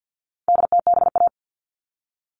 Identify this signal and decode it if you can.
Morse code, heard as a keyed tone, about 50 wpm, 700 Hz. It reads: DTBA
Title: All Cut Numbers - 4-Digit 50wpm